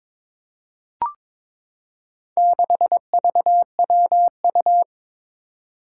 Morse code, heard as a keyed tone, about 22 wpm, 700 Hz. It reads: 6VWU
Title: Four Challenging Letter-Number Combinations 22wpm